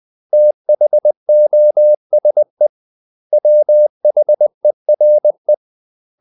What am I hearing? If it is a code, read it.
Morse code, heard as a keyed tone, about 20 wpm, 600 Hz. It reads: THOSE WHERE
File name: CW / Practice / 500MostCommonEnglishWords20